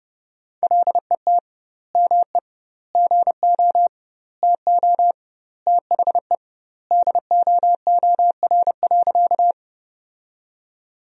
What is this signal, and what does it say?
Morse code, heard as a keyed tone, about 31 wpm, 700 Hz. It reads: LET ME GO TO THE DOOR.
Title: 2x Word-Spacing - Sentences from Top 300 Words - 30wpm